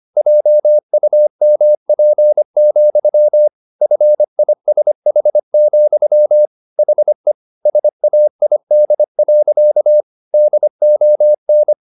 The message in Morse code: JUMP, FISH, HE SAID. DON

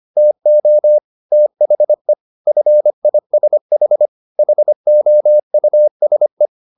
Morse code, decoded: TO THE FISH HOUSE

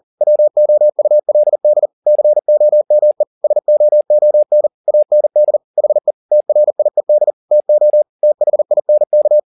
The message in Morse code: WOULD COME SOON AND HE TRIED TO THINK

